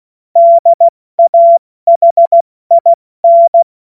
Morse code, heard as a keyed tone, about 16 wpm, spaced as for 14 wpm, 680 Hz. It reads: DAHIN
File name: Morsekurs / Witze / Witz 23. Dezember 2025 70 F80